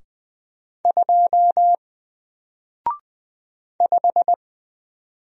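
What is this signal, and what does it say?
Morse code, heard as a keyed tone, about 20 wpm, 700 Hz. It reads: 2 5
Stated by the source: Single letter and number Morse drill at 20 wpm, from N4LCD.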